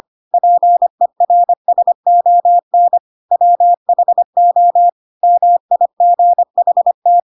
Morse code, heard as a keyed tone, about 25 wpm, 700 Hz. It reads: PERSON WHO MIGHT
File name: Pinocchio / Pinocchio ch5 25 25 ik1hge